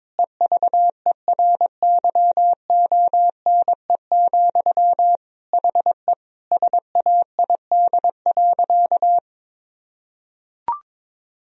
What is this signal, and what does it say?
Morse code, heard as a keyed tone, about 21 wpm, 700 Hz. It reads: EVERYONE, HE SAID.
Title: Sentences from Top 1000 Words - Review Part 3 - 22wpm